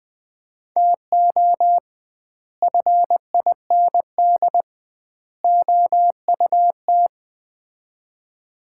Morse code, heard as a keyed tone, about 20 wpm, 700 Hz. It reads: TO FIND OUT
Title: 2x Word-Spacing - Sets of 3 Words - 20wpm